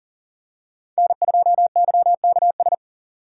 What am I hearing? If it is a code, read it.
Morse code, heard as a keyed tone, about 40 wpm, 700 Hz. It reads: N2YKS
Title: USA Call Signs - 5 Characters - 2x Character Spacing - 40wpm